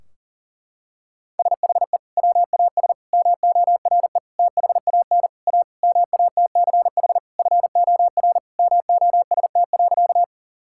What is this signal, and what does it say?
Morse code, heard as a keyed tone, about 40 wpm, 700 Hz. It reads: SHE WAS MORE THAN A MATCH FOR MOST.
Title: Sentences from Top 1600 Words 40wpm